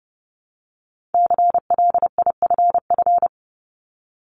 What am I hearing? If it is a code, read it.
Morse code, heard as a keyed tone, about 30 wpm, 700 Hz. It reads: CLIFF
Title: Multi-letter Phonemes Repeated 3x 30wpm